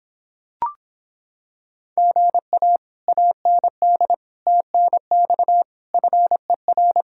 Morse code, heard as a keyed tone, about 26 wpm, 700 Hz. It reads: GA AND TNX FER